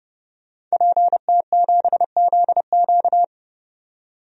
Morse code, heard as a keyed tone, about 32 wpm, 700 Hz. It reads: PT7ZQ
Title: DXCC-100 Call Signs - 5 Characters - 30wpm